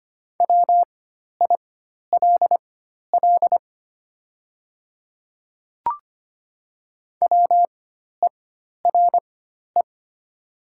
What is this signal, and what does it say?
Morse code, heard as a keyed tone, about 25 wpm, spaced as for 12 wpm, 700 Hz. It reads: WILL WERE
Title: 4x Character Spacing - Top 100 Words - 25wpm